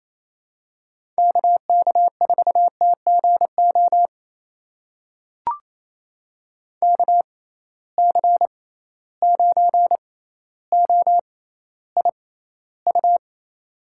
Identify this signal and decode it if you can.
Morse code, heard as a keyed tone, about 28 wpm, spaced as for 14 wpm, 700 Hz. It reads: KK4TGO KC9OIU
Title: USA Call Signs - 6 Characters - 6x Character Spacing - 28wpm